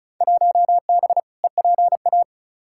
Morse code, heard as a keyed tone, about 36 wpm, 700 Hz. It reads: 1B EPA
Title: Field Day Exchanges Encore - 35wpm